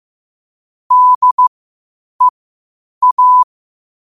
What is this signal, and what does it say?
Morse code, heard as a keyed tone, about 15 wpm, spaced as for 8 wpm, 1000 Hz. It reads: DEA
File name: qotdfiles / 8wpm